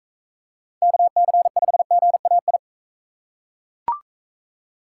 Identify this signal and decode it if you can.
Morse code, heard as a keyed tone, about 37 wpm, 700 Hz. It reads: KC5GAI